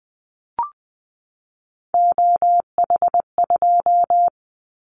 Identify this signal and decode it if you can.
Morse code, heard as a keyed tone, about 20 wpm, 700 Hz. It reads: OH2
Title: Callsign Prefixes for Top 50 DXCC - 20wpm